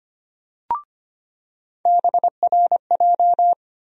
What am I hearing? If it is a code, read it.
Morse code, heard as a keyed tone, about 25 wpm, 700 Hz. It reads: BRJ
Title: USA Call Sign Postfixes - 3 Letters 25wpm